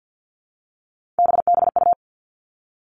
Morse code, heard as a keyed tone, about 50 wpm, 700 Hz. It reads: BBU
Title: Three Challenging Letter-Number Combinations 50wpm